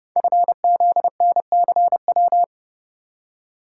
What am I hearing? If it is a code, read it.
Morse code, heard as a keyed tone, about 30 wpm, 700 Hz. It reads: FZNCW
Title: Code Groups - 5 Letters - 30wpm